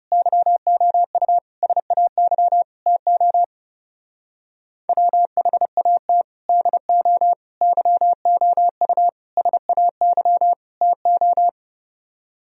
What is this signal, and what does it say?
Morse code, heard as a keyed tone, about 29 wpm, 700 Hz. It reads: YOU SAY TO WHAT DO YOU SAY TO
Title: Speed-Racing - Sets of 5 Words 30wpm